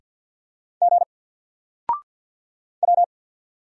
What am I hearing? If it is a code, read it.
Morse code, heard as a keyed tone, about 54 wpm, 700 Hz. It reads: G W
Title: Single Letter-Number Alt 50wpm